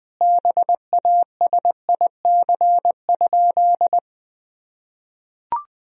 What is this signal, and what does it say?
Morse code, heard as a keyed tone, about 20 wpm, 700 Hz. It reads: BASIC?
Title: Sentences from Top 1000 Words - Review Part 3 - 20wpm